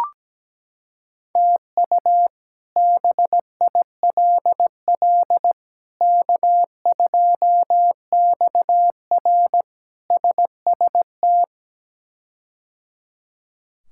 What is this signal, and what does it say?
Morse code, heard as a keyed tone, about 17 wpm, 700 Hz. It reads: TU BILL K2XR SST
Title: K1USN SST QSO Practice - 17wpm